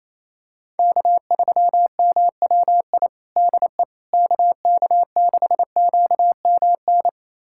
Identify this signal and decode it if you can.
Morse code, heard as a keyed tone, about 28 wpm, 700 Hz. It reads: K3MWI DE KK6QMN
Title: USA Callsign DE Callsign 28wpm